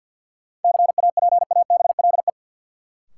Morse code, heard as a keyed tone, about 51 wpm, 700 Hz. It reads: CAPABLE
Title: Sentences from Top 1200 Words - New Word Drill Alt - 50wpm